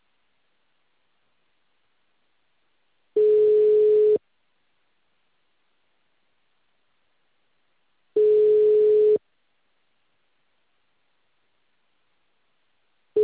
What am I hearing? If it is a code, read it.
Ringback tone (Europe, 425 Hz).